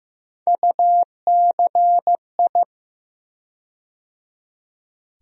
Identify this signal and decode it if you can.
Morse code, heard as a keyed tone, about 15 wpm, 700 Hz. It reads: UCI
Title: USA Call Sign Postfixes - 3 Letters 15wpm